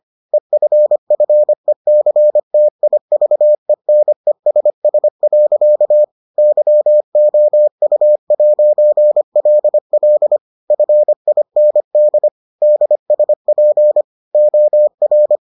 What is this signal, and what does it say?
Morse code, heard as a keyed tone, about 25 wpm, 600 Hz. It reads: EFFECTIVENESS. YOU'LL FIND DSP OR